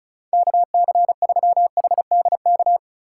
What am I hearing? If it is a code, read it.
Morse code, heard as a keyed tone, about 34 wpm, 700 Hz. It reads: KC3HDK
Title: USA Call Signs - 6 Characters - 3x Character Spacing - 35wpm